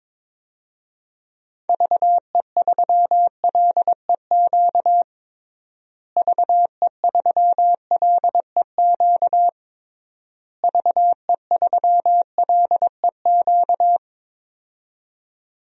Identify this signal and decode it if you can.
Morse code, heard as a keyed tone, about 22 wpm, 700 Hz. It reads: VE3LEQ VE3LEQ VE3LEQ
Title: DXCC-100 Call Signs - 6 Characters - Repeated 3x - 22wpm